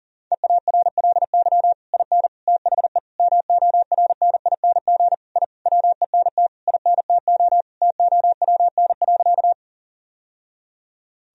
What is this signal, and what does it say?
Morse code, heard as a keyed tone, about 37 wpm, 700 Hz. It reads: EARLY IN THE MORNING I WENT INTO TOWN.
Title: Sentences from Top 1000 Words - Review Part 2 - 40wpm